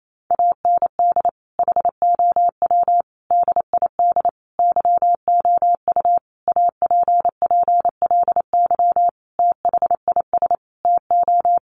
Morse code, heard as a keyed tone, about 28 wpm, 700 Hz. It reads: AND HOW DID YOU APPLY THIS TO
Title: Sentences from Top 1200 Words 28wpm